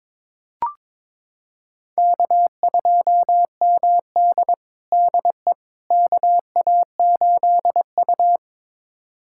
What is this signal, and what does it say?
Morse code, heard as a keyed tone, about 22 wpm, 700 Hz. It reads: K2MD DE KA8U
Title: USA Callsign DE Callsign - 4 Characters - 22wpm